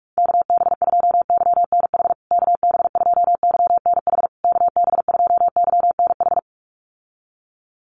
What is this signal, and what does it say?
Morse code, heard as a keyed tone, about 44 wpm, 700 Hz. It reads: KB2YNH KB2YNH KB2YNH
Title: USA Call Signs Repeated 3x 45wpm